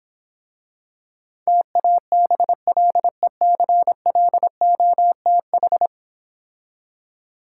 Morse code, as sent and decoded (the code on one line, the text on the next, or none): - .- -... .-.. -.-. .-.. --- - ....
TABLECLOTH